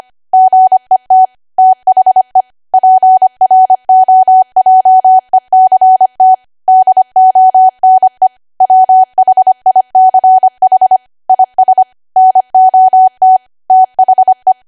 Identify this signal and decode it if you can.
Morse code, heard as a keyed tone, about 25 wpm, 750 Hz. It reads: GET THE PROJECT DONE WHICH IS NOT THE
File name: Archive / 25 WPM / 25WPM